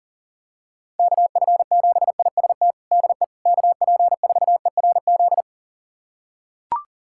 Morse code, heard as a keyed tone, about 40 wpm, 700 Hz. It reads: KF7IST DE KP4ERZ
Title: USA Callsign DE Callsign - 6 Characters - 40wpm